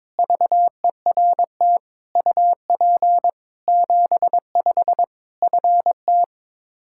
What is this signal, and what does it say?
Morse code, heard as a keyed tone, about 22 wpm, 700 Hz. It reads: VERT UP 75 FT
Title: Practice QSO - Exchange 2 - Info - 22wpm